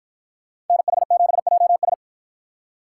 Morse code, heard as a keyed tone, about 53 wpm, 700 Hz. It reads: NH7JS